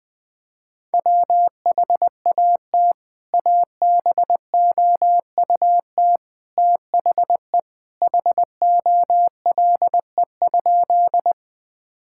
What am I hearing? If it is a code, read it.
Morse code, heard as a keyed tone, about 21 wpm, 700 Hz. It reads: WHAT ABOUT THE HOLE?
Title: Sentences from Top 1600 Words 20wpm